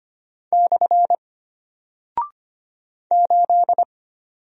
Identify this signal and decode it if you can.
Morse code, heard as a keyed tone, about 25 wpm, 700 Hz. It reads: / 8
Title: Single Character (Letter:Number:Punctuation:Prosign) Alt 25wpm